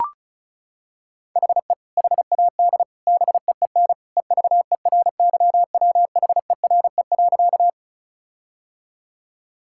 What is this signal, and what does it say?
Morse code, heard as a keyed tone, about 36 wpm, 700 Hz. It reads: HE HAD BEEN EVERYWHERE.
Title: Sentences from Top 1600 Words 35wpm